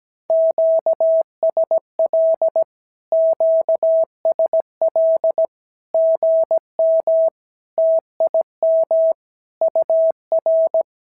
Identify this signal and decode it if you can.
Morse code, heard as a keyed tone, about 17 wpm, 650 Hz. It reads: QSL QSL GM TIM UR
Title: SOTA QSO Practice 17wpm